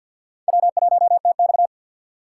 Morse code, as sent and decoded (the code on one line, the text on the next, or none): .-- .---- - -..-
W1TX